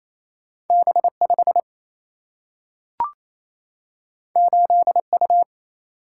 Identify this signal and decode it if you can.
Morse code, heard as a keyed tone, about 28 wpm, 700 Hz. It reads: B5 8U
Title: Single-Letter Single-Number 28wpm